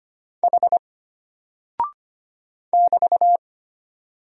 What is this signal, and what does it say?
Morse code, heard as a keyed tone, about 25 wpm, 700 Hz. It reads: H <BT>
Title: Single Character (Letter:Number:Punctuation:Prosign) Alt 25wpm